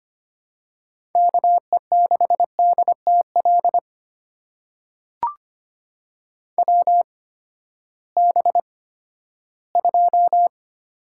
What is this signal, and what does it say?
Morse code, heard as a keyed tone, about 25 wpm, spaced as for 14 wpm, 700 Hz. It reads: KE6DTL WB2
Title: USA Call Signs - 6 Characters - 8x Character Spacing - 25wpm